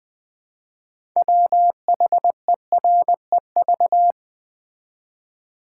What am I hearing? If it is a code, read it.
Morse code, heard as a keyed tone, about 20 wpm, 700 Hz. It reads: WHEREV
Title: Build A Word - 3 Letters At A Time - 8-Letter Words - 20wpm